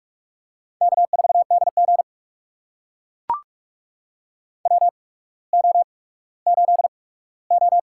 Morse code, heard as a keyed tone, about 42 wpm, spaced as for 19 wpm, 700 Hz. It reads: K4DG WO8O
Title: USA Call Signs - 4 Characters - 8x Character Spacing - 45wpm